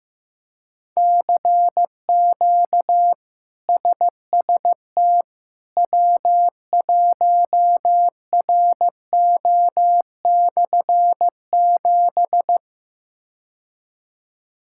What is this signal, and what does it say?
Morse code, heard as a keyed tone, about 15 wpm, 700 Hz. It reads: CQ SST W1RO/7
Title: K1USN SST QSO Practice - 15wpm